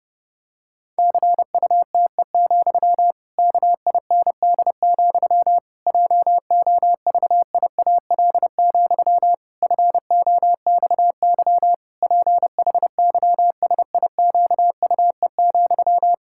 Morse code, heard as a keyed tone, about 32 wpm, 700 Hz. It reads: CUTE, KIND, JOVIAL, FOXY PHYSIQUE,
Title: Pangram Challenge v2 - 30wpm